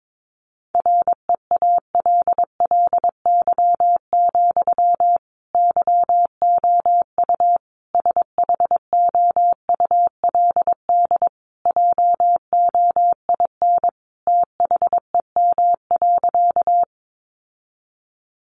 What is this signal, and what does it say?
Morse code, heard as a keyed tone, about 22 wpm, 700 Hz. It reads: REALLY, YOU SHOULD JOIN THEM.
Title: Sentences from Top 1400 Words 22wpm